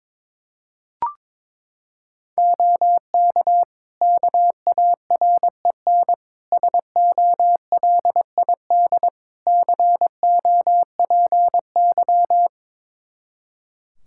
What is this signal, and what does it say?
Morse code, heard as a keyed tone, about 22 wpm, 700 Hz. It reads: OK KAREN SOLID COPY
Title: Practice QSO - Exchange 3 - More Info - 22wpm